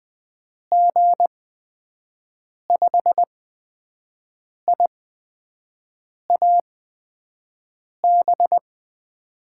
Morse code, heard as a keyed tone, about 21 wpm, spaced as for 5 wpm, 700 Hz. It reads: G5IAB
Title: USA Call Signs - 6 Characters - 8x Character Spacing - 20wpm